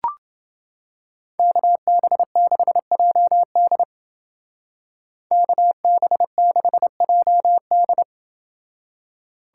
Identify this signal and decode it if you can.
Morse code, heard as a keyed tone, about 29 wpm, 700 Hz. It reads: KB6JD KB6JD